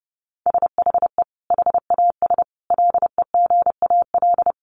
Morse code, heard as a keyed tone, about 30 wpm, 700 Hz. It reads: SHE HAS LEGAL
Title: Sentences from Top 1200 Words 30wpm